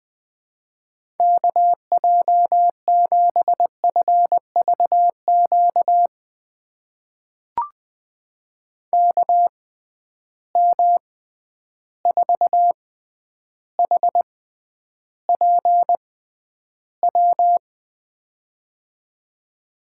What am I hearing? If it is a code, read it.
Morse code, heard as a keyed tone, about 20 wpm, spaced as for 10 wpm, 700 Hz. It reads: KJ7FVQ KM4HPW
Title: USA Call Signs - 6 Characters - 6x Character Spacing - 20wpm